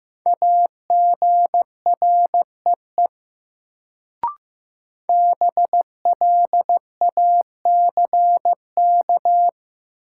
Morse code, heard as a keyed tone, about 15 wpm, 700 Hz. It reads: AGREE BLACK
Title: Sets of 5 Words Encore - New Word Drill - 15wpm